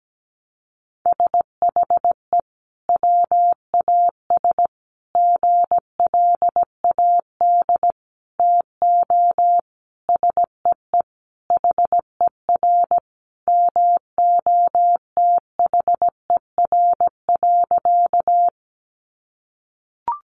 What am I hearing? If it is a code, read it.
Morse code, heard as a keyed tone, about 17 wpm, 700 Hz. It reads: SHE WAS GLAD TO SEE HER MOTHER.
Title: Sentences from Top 800 Words 17wpm